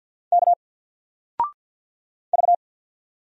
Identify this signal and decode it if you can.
Morse code, heard as a keyed tone, about 50 wpm, 700 Hz. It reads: K V
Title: Single Letter-Number 50wpm